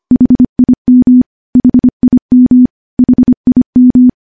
Morse code, heard as a keyed tone, about 25 wpm, 250 Hz. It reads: HIM HIM HIM
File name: morse-training-25WPM